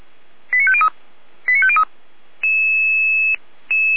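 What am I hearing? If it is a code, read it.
Five-tone selective call (ZVEI), heard as quick runs of five tones, tones of about 70 ms. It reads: 89503 89503
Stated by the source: ZVEI five-tone call 89160 received by SDR, recorded by feuerwehr-orlamuende (fire brigade Orlamünde).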